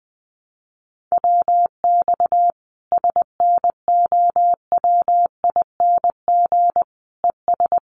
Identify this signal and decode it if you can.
Morse code, heard as a keyed tone, about 20 wpm, 700 Hz. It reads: WX SNOWING ES
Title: Common QSO Phrases 20wpm